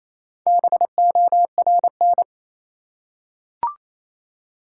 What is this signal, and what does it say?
Morse code, heard as a keyed tone, about 28 wpm, 700 Hz. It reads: BORN
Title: Four Letter Words - 28wpm